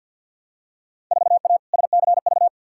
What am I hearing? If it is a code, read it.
Morse code, heard as a keyed tone, about 47 wpm, 700 Hz. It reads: 4A SCV
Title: Field Day Exchanges Encore - 50wpm